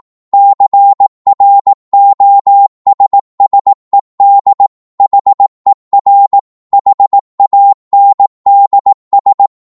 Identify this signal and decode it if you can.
Morse code, heard as a keyed tone, about 18 wpm, 800 Hz. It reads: CROSSED HER HANDS